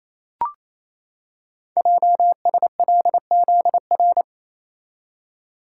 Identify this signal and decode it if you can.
Morse code, heard as a keyed tone, about 27 wpm, 700 Hz. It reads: JSLZR
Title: Code Groups - 5 Letters - 28wpm